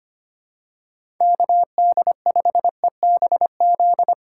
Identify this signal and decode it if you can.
Morse code, heard as a keyed tone, about 25 wpm, 700 Hz. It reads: KD5EBZ
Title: USA Callsign DE Callsign - 6 Characters - 25wpm